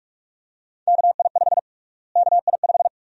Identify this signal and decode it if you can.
Morse code, heard as a keyed tone, about 43 wpm, 690 Hz. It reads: KI5 KI5
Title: USA Call Sign Prefixes - Repeated 2x - 45wpm